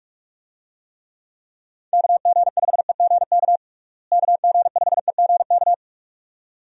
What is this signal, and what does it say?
Morse code, heard as a keyed tone, about 46 wpm, 700 Hz. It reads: KG5EGK KG5EGK
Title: USA Call Signs - 6 Characters - Repeated 2x - 45wpm